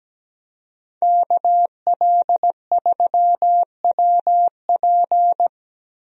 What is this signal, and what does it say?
Morse code, heard as a keyed tone, about 17 wpm, 700 Hz. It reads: KL3WP
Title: USA Call Signs - 5 Characters - Repeated 2x - 17wpm